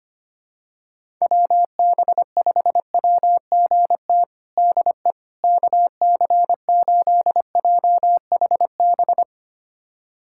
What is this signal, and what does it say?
Morse code, heard as a keyed tone, about 25 wpm, 700 Hz. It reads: WB5WGT DE KC8JHB
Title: USA Callsign DE Callsign 25wpm